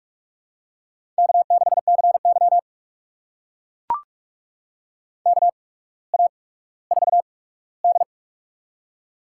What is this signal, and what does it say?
Morse code, heard as a keyed tone, about 44 wpm, spaced as for 19 wpm, 690 Hz. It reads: K6CY KA4D
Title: USA Call Signs - 4 Characters - 8x Character Spacing - 45wpm